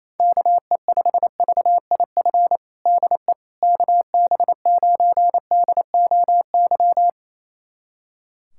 Morse code, heard as a keyed tone, about 27 wpm, 700 Hz. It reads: KE5VIF DE KB9DOY
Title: USA Callsign DE Callsign - 6 Characters - 28wpm